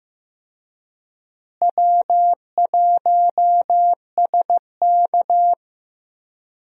Morse code, heard as a keyed tone, about 15 wpm, 700 Hz. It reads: W1SK
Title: USA Call Signs - 4 Characters - Repeated 3x - 15wpm